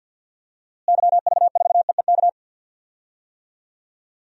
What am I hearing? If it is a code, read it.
Morse code, heard as a keyed tone, about 50 wpm, 700 Hz. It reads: YV4EEK
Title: DXCC-100 Call Signs - 6 Characters - Repeated 2x - 50wpm